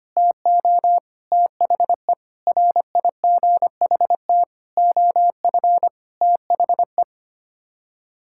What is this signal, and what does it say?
Morse code, heard as a keyed tone, about 25 wpm, 700 Hz. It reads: TO THE RIGHT OF THE
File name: Sets of 9 Words - 25wpm